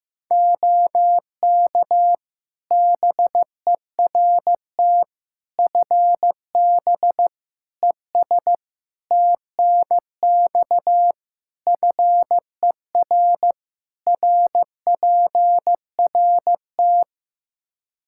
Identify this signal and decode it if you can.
Morse code, heard as a keyed tone, about 15 wpm, 700 Hz. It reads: OK BERT FB ES TNX FER RPRT
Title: Practice QSO - Exchange 2 - Info - 15wpm